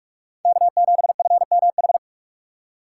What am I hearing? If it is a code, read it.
Morse code, heard as a keyed tone, about 43 wpm, 700 Hz. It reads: K7FMH
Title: USA Call Signs - 3x Character Spacing - 45wpm